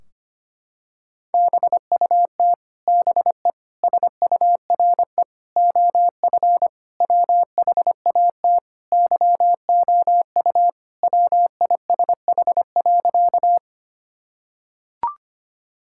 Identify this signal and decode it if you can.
Morse code, heard as a keyed tone, about 25 wpm, 700 Hz. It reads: BUT BE SURE OF WHAT YOU WISH.
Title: Sentences from Top 1600 Words 25wpm